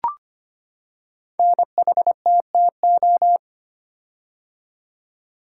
Morse code, heard as a keyed tone, about 25 wpm, 700 Hz. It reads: NHTTO
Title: Code Groups - 5 Letters - 25wpm